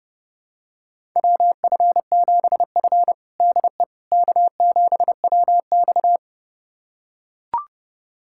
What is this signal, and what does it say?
Morse code, heard as a keyed tone, about 30 wpm, 700 Hz. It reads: WF7F DE K7WX
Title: USA Callsign DE Callsign - 4 Characters - 30wpm